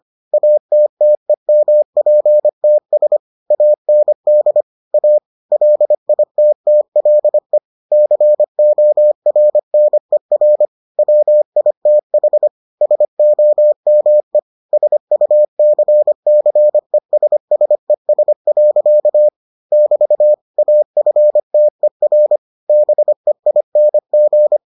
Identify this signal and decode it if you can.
Morse code, heard as a keyed tone, about 25 wpm, 600 Hz. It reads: ATTEMPTS AND A LITTLE CORNER WITH SOME SUCCESSES. <BT> AFTER BEING